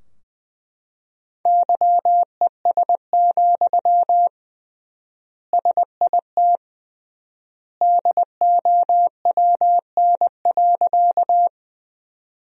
Morse code, heard as a keyed tone, about 21 wpm, 700 Hz. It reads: YES, SIT DOWN.
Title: 3x Word-Spacing - Sentences from Top 500 Words - 20wpm